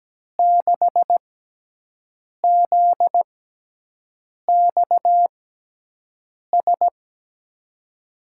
Morse code, heard as a keyed tone, about 17 wpm, spaced as for 6 wpm, 700 Hz. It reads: 6ZXS
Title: USA Call Signs - 6x Character Spacing - 17wpm